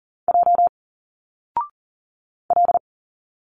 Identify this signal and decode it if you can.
Morse code, heard as a keyed tone, about 40 wpm, 700 Hz. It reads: J L
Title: Single Letters 40wpm